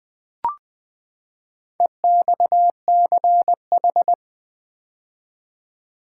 Morse code, heard as a keyed tone, about 20 wpm, 700 Hz. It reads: EXCH
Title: Build A Word - 4 Letters At A Time - 8-Letter Words - 20wpm